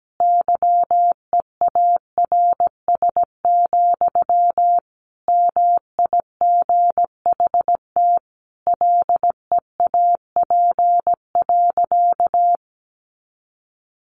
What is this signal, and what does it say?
Morse code, heard as a keyed tone, about 17 wpm, 700 Hz. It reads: YEARS, MIGHT LEAP.
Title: A Sound Of Thunder 17wpm